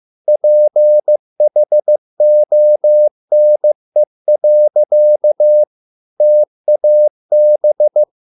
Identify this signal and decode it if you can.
Morse code, heard as a keyed tone, about 15 wpm, 600 Hz. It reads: PHONE. TAB